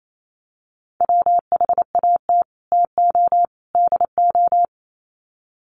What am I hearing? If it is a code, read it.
Morse code, heard as a keyed tone, about 28 wpm, 700 Hz. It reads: WHAT TO DO
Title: Sets of 3 Words 28wpm